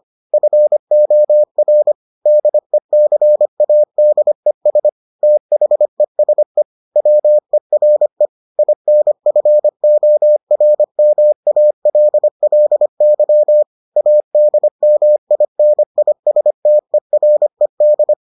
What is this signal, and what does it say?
Morse code, heard as a keyed tone, about 25 wpm, 600 Hz. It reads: FOR DECADES THESE WERE INFORMALLY ADMINISTERED